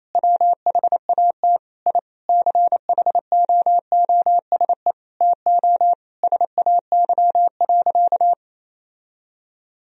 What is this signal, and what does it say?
Morse code, heard as a keyed tone, about 28 wpm, 700 Hz. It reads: WHAT I CHOOSE TO SAY.
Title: Sentences from Top 1000 Words - Review Part 4 - 28wpm